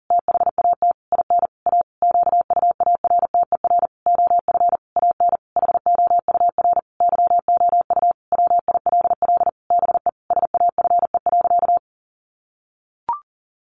Morse code, heard as a keyed tone, about 40 wpm, 700 Hz. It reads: THAT IN A QUARTER OF AN HOUR YOU WILL BE SAFE.
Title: Sentences from Top 1300 Words 40wpm